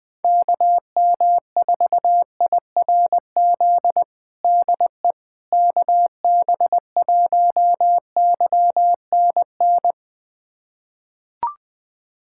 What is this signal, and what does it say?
Morse code, heard as a keyed tone, about 20 wpm, 700 Hz. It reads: KM4IRZ DE KB1YNN
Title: USA Callsign DE Callsign - 6 Characters - 20wpm